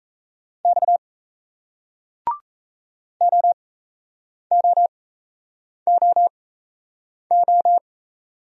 Morse code, heard as a keyed tone, about 38 wpm, spaced as for 8 wpm, 700 Hz. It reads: X OOOO